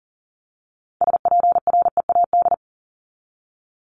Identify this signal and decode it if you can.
Morse code, heard as a keyed tone, about 40 wpm, 700 Hz. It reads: SPREAD
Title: Top 1001-1250 Words 40wpm